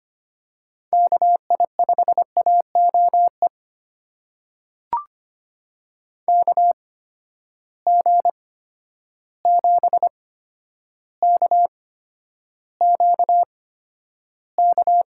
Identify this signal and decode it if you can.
Morse code, heard as a keyed tone, about 25 wpm, spaced as for 11 wpm, 700 Hz. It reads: KI5AOE KG7KQK
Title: USA Call Signs - 8x Character Spacing - 25wpm